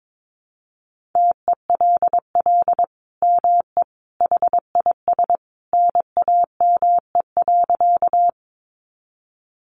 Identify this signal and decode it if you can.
Morse code, heard as a keyed tone, about 22 wpm, 700 Hz. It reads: TELL ME HIS NAME.
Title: Sentences from Top 200 Words 22wpm